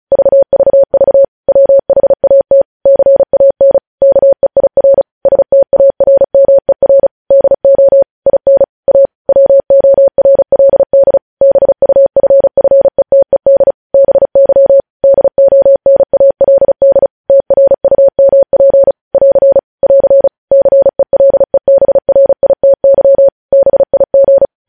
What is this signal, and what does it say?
Morse code, heard as a keyed tone, about 36 wpm, 550 Hz. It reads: VVV WHAT CAN KEIR STARMER DO IN A WORLD BUFFETED BY DONALD TRUMP <AR> <AR> CELEBRITY BIG
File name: podcast / News headlines-1x-35x35